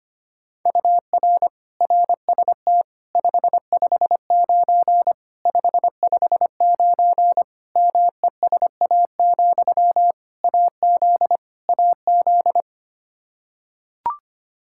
Morse code, heard as a keyed tone, about 25 wpm, 700 Hz. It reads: UR RST 559 559 MESA, AZ AZ
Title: Practice QSO - Exchange 1 - Beginning - 25wpm